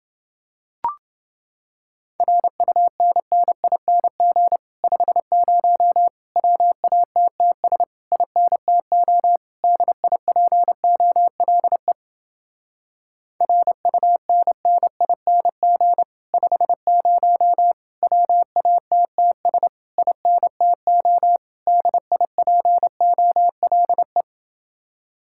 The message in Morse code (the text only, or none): RUNNING 50 WATTS INTO DIPOLE RUNNING 50 WATTS INTO DIPOLE